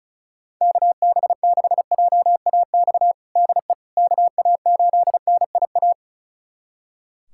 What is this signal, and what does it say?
Morse code, heard as a keyed tone, about 36 wpm, 700 Hz. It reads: KB6JAX DE KA8NIA
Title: USA Callsign DE Callsign - 6 Characters - 35wpm